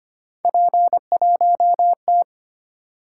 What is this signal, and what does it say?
Morse code, heard as a keyed tone, about 25 wpm, 700 Hz. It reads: P1T
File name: Three Letter-Number Combinations - 25wpm